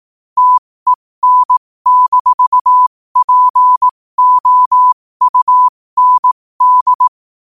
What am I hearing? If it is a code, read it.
Morse code, heard as a keyed tone, about 18 wpm, spaced as for 15 wpm, 1000 Hz. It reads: TEN-POUND